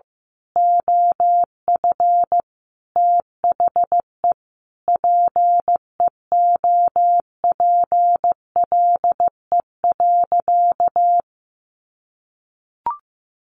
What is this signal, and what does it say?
Morse code, heard as a keyed tone, about 15 wpm, 700 Hz. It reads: OF THE PEOPLE.
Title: Sentences from Top 1500 Words 15wpm